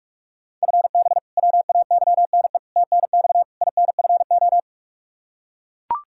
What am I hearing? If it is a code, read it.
Morse code, heard as a keyed tone, about 41 wpm, 700 Hz. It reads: FB WAYNE TNX INFO
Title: Practice QSO - Exchange 3 - More Info - 45wpm